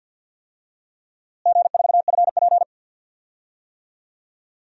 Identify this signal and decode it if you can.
Morse code, heard as a keyed tone, about 47 wpm, 700 Hz. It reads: G4FP
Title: DXCC-100 Call Signs - 4 Characters - Repeated 2x - 50wpm